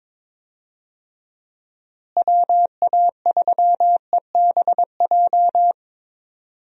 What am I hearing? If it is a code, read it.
Morse code, heard as a keyed tone, about 21 wpm, 700 Hz. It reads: WA3EBJ
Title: USA Call Signs - 6 Characters - Repeated 3x - 22wpm